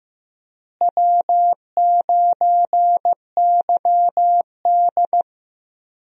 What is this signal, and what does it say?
Morse code, heard as a keyed tone, about 15 wpm, 700 Hz. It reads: W9YD
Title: USA Call Signs - 4 Characters - Repeated 3x - 15wpm